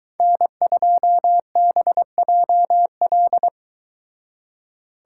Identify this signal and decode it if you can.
Morse code, heard as a keyed tone, about 23 wpm, 700 Hz. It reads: N2BJL